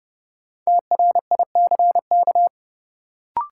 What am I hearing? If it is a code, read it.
Morse code, heard as a keyed tone, about 32 wpm, 700 Hz. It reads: TRICK
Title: Sentences from Top 1500 Words - New Word Drill Alt - 30wpm